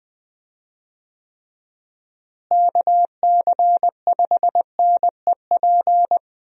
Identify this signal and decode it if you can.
Morse code, heard as a keyed tone, about 20 wpm, 700 Hz. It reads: KC5NEP
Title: USA Call Signs - 6 Characters - Repeated 2x - 20wpm